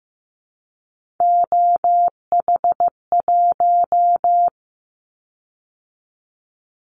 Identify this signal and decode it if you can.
Morse code, heard as a keyed tone, about 15 wpm, 700 Hz. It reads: OH1
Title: Callsign Prefixes for Top 25 DXCC - 15wpm